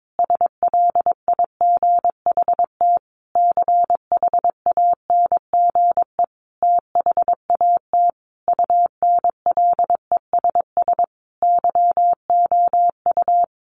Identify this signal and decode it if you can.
Morse code, heard as a keyed tone, about 22 wpm, 700 Hz. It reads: SLIGHT CHANGE THAT UNLESS YOU
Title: A Sound Of Thunder 22wpm